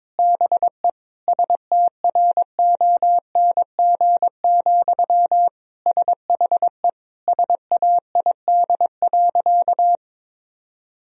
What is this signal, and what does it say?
Morse code, heard as a keyed tone, about 21 wpm, 700 Hz. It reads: BE STRONG, SHE SAID.
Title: Sentences from Top 1000 Words - Review Part 3 - 22wpm